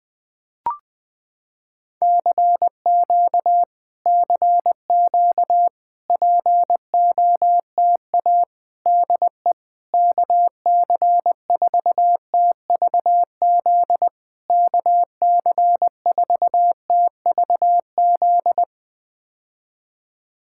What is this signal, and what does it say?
Morse code, heard as a keyed tone, about 20 wpm, 700 Hz. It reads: CQ CQ POTA DE KC4TVZ KC4TVZ
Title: POTA QSO Practice - 20wpm